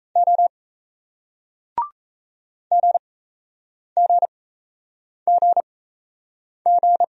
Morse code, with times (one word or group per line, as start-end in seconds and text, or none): 0.2-0.5 O
2.7-7.0 GGGG